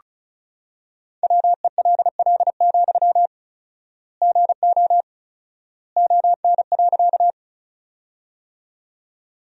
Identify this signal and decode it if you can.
Morse code, heard as a keyed tone, about 36 wpm, 700 Hz. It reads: WELL, GO ON.
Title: 4x Word-Spacing - Sentences from Top 200 Words - 35wpm